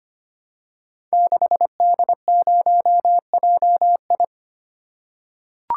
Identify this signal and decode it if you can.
Morse code, heard as a keyed tone, about 25 wpm, 690 Hz. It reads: 6D0JI
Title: Code Groups - 5 Letter:Numbers - 25wpm